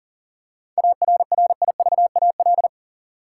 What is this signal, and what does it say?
Morse code, heard as a keyed tone, about 40 wpm, 700 Hz. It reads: ARRIVAL
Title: Seven Letter Words - 40wpm